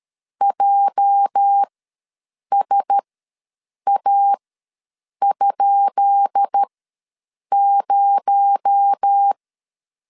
Morse code, heard as a keyed tone, about 13 wpm, spaced as for 7 wpm, 800 Hz. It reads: JSA?0
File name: LEZIONI / 10 SIGNS / 10B INT ws 12wpm 3527